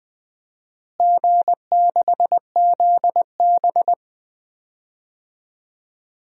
Morse code, heard as a keyed tone, about 20 wpm, 700 Hz. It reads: G6ZB
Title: DXCC-100 Call Signs - 4 Characters - Repeated 3x - 20wpm